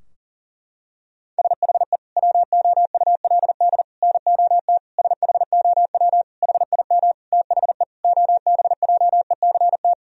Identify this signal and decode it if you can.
Morse code, heard as a keyed tone, about 40 wpm, 700 Hz. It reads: SHE WOULD NOT SHOW HIM THE OBJECT
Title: Sentences from Top 1000 Words - Review Part 3 - 40wpm